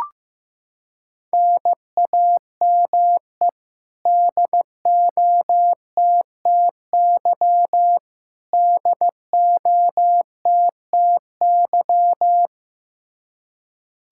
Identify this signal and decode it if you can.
Morse code, heard as a keyed tone, about 15 wpm, 700 Hz. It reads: NAME DOTTY DOTTY
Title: Practice QSO - Exchange 1 - Beginning - 15wpm